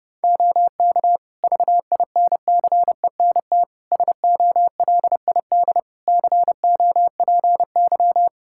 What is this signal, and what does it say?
Morse code, heard as a keyed tone, about 29 wpm, 700 Hz. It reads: OK VINCENT SOLID COPY
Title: Practice QSO - Exchange 3 - More Info - 30wpm